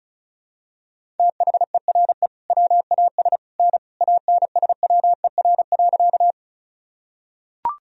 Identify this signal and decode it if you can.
Morse code, heard as a keyed tone, about 36 wpm, 700 Hz. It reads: THERE WAS N ANSWER.
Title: Sentences from Top 300 Words and Abbreviations 35wpm